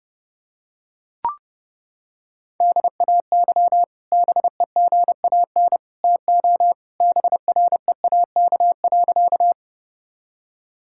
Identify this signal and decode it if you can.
Morse code, heard as a keyed tone, about 30 wpm, 700 Hz. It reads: DAY BEGAN TO BREAK.
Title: Sentences from Top 1000 Words - Review Part 6 - 30wpm